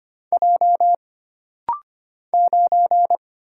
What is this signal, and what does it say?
Morse code, heard as a keyed tone, about 25 wpm, 700 Hz. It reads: J 9
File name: Single Character (Letter:Number:Punctuation:Prosign) - Rapid-Fire - 25wpm